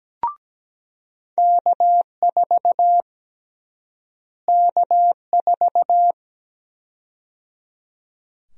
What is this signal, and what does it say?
Morse code, heard as a keyed tone, about 17 wpm, 700 Hz. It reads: K4 K4
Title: USA Call Sign Prefixes - Repeated 2x - 17wpm